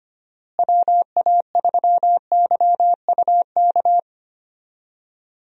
Morse code, heard as a keyed tone, about 25 wpm, 700 Hz. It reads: WA3YUK
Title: USA Call Signs - 6 Characters - Repeated 2x - 25wpm